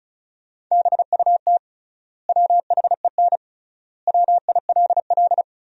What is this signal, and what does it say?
Morse code, heard as a keyed tone, about 36 wpm, 700 Hz. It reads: BUT WHEN WILL
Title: 3x Word-Spacing - Sentences from Top 100 Words - 35wpm